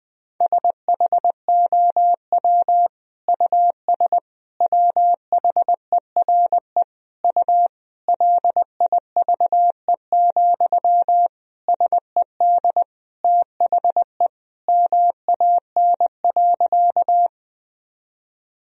Morse code, heard as a keyed tone, about 20 wpm, 700 Hz. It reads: SHOW US WHERE U LIVE, SED THE MAN.
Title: Sentences from Top 200 Words and Abbreviations 20wpm